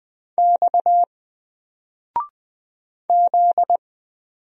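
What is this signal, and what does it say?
Morse code, heard as a keyed tone, about 20 wpm, 700 Hz. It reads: X Z
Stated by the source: Single letter and number Morse drill at 20 wpm, from N4LCD.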